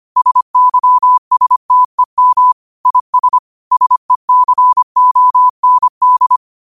Morse code, heard as a keyed tone, about 25 wpm, 1000 Hz. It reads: SYSTEM IS SECOND
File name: qotdfiles / 25wpm